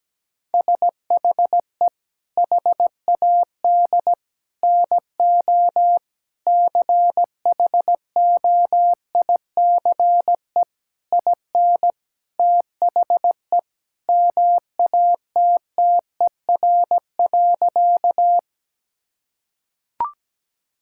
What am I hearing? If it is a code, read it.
Morse code, heard as a keyed tone, about 17 wpm, 700 Hz. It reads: SHE HAD NO CHOICE IN THE MATTER.
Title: Sentences from Top 1000 Words - Review Part 6 - 17wpm